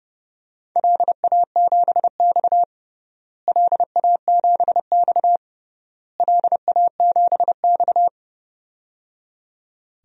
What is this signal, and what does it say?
Morse code, heard as a keyed tone, about 31 wpm, 700 Hz. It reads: LA7X LA7X LA7X
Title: DXCC-100 Call Signs - 4 Characters - Repeated 3x - 30wpm